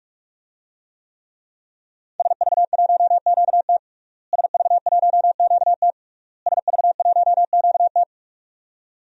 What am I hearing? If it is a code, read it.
Morse code, heard as a keyed tone, about 46 wpm, 700 Hz. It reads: SV1QT SV1QT SV1QT
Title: DXCC-100 Call Signs - 5 Characters - Repeated 3x - 45wpm